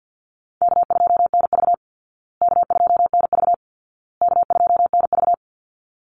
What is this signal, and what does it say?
Morse code, heard as a keyed tone, about 48 wpm, 700 Hz. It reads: K2NV K2NV K2NV
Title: Call Signs Repeated 2x - 50wpm